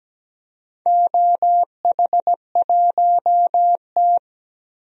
Morse code, heard as a keyed tone, about 17 wpm, 700 Hz. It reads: OH1T
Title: DXCC-100 Call Signs - 4 Characters - Repeated 3x - 17wpm